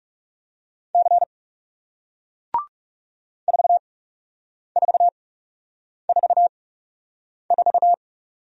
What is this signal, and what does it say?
Morse code, heard as a keyed tone, about 39 wpm, spaced as for 8 wpm, 700 Hz. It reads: C 4444